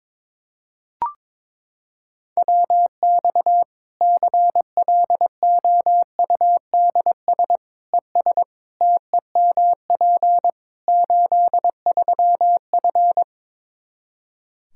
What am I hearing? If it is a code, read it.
Morse code, heard as a keyed tone, about 22 wpm, 700 Hz. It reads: WX CLOUDS ES TEMP 83F
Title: Practice QSO - Exchange 2 - Info - 22wpm